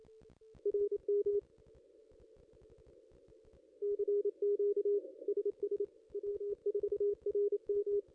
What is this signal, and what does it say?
Morse code, heard as a keyed tone, about 28 wpm, 410 Hz. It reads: RM CQ SS W4RM
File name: contests / ss-cw-k5tr-op-n5rz -0200